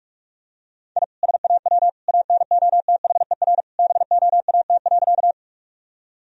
Morse code, heard as a keyed tone, about 42 wpm, 700 Hz. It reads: I SAW ANOTHER BOAT.
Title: Sentences from Top 1000 Words - Review Part 3 - 45wpm